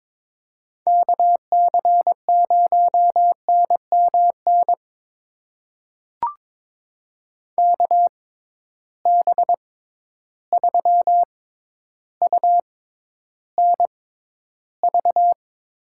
Morse code, heard as a keyed tone, about 21 wpm, spaced as for 11 wpm, 700 Hz. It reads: KC0NMN KB3UNV
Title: USA Call Signs - 6 Characters - 6x Character Spacing - 22wpm